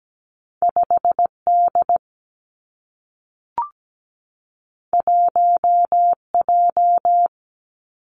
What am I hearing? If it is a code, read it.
Morse code, heard as a keyed tone, about 17 wpm, 700 Hz. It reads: 5 D 1 J
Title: Two Challenging Letter-Number Combinations 17wpm